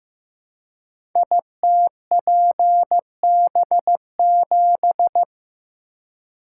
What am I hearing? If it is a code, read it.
Morse code, heard as a keyed tone, about 15 wpm, 700 Hz. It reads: ITPB7
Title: Code Groups - 5 Letter:Numbers - 15wpm